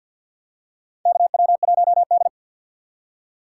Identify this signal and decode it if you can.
Morse code, heard as a keyed tone, about 51 wpm, 700 Hz. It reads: KW1D
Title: USA Call Signs - 4x Character Spacing - 50wpm